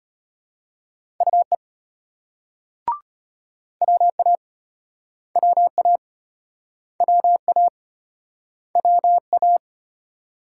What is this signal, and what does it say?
Morse code, heard as a keyed tone, about 30 wpm, 700 Hz. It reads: UE WA WA WA WA
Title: Speed-Racing Multi-letter Phonemes - 25wpm